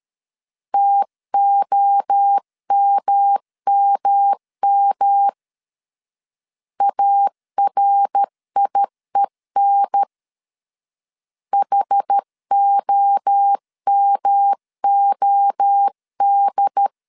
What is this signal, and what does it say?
Morse code, heard as a keyed tone, about 13 wpm, 800 Hz. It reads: TOMMM ARIEN HOMOD
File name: CW / DET / 1 3 1500C S A 12wpm 2905